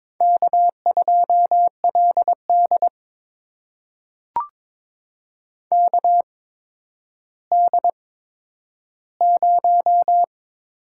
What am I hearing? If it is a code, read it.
Morse code, heard as a keyed tone, about 22 wpm, spaced as for 11 wpm, 700 Hz. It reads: K2LD KD0
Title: USA Call Signs - 4 Characters - 8x Character Spacing - 22wpm